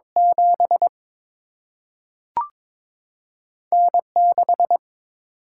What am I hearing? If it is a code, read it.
Morse code, heard as a keyed tone, about 21 wpm, 700 Hz. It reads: 7 N 6
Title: Single-Letter Single-Number 22wpm